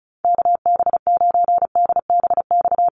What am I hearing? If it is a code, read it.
Morse code, heard as a keyed tone, about 34 wpm, 700 Hz. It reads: KB9DBX
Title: USA Call Signs Repeated 3x 35wpm